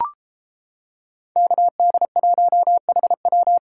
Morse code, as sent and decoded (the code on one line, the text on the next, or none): -.- -.. .---- .... .--
KD1HW